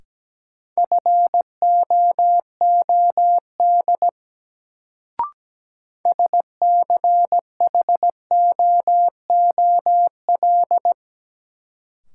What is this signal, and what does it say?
Morse code, heard as a keyed tone, about 17 wpm, 700 Hz. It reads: FOOD SCHOOL
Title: Sentences from Top 300 Words - New Word Drill - 17wpm